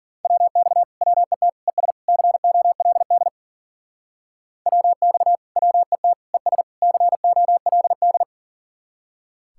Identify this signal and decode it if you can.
Morse code, heard as a keyed tone, about 52 wpm, 700 Hz. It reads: WX WET ES COLD WX WET ES COLD